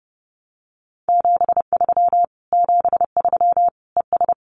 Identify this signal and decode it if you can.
Morse code, heard as a keyed tone, about 30 wpm, 700 Hz. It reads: 73 73 ES
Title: Common QSO Phrases Encore 30wpm